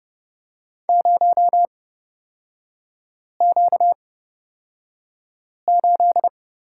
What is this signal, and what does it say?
Morse code, heard as a keyed tone, about 31 wpm, spaced as for 5 wpm, 700 Hz. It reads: 0Q8
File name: Lesson 38e - Letter q - Three Characters 30wpm